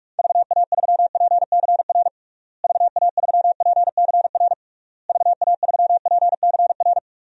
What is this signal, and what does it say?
Morse code, heard as a keyed tone, about 46 wpm, 700 Hz. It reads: VA3PCR VA3PCR VA3PCR
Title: DXCC-100 Call Signs - 6 Characters - Repeated 3x - 45wpm